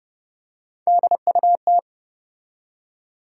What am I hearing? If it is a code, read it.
Morse code, heard as a keyed tone, about 30 wpm, 700 Hz. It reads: DUT
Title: USA Call Sign Postfixes - 3 Letters 30wpm